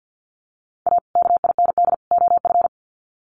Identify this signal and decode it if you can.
Morse code, heard as a keyed tone, about 50 wpm, 700 Hz. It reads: A KIND OF
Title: Sets of 3 Words 50wpm